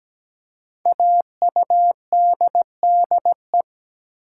Morse code, heard as a keyed tone, about 17 wpm, 700 Hz. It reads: AUDDE
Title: Code Groups - 5 Letters - 17wpm